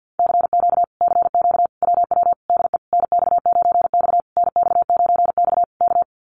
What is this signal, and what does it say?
Morse code, heard as a keyed tone, about 50 wpm, 700 Hz. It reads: CQ CQ WW DE NX9X NX9X K
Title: Common QSO Phrases Encore 50wpm